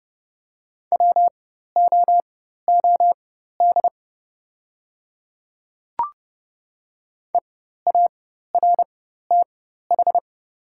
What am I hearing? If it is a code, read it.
Morse code, heard as a keyed tone, about 30 wpm, spaced as for 14 wpm, 700 Hz. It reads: WOOD EARTH
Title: 4x Character Spacing - Top 500 Words - 30wpm